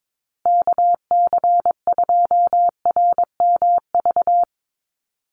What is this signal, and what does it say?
Morse code, heard as a keyed tone, about 22 wpm, 700 Hz. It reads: KC2RMV
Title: USA Call Signs Repeated 3x 22wpm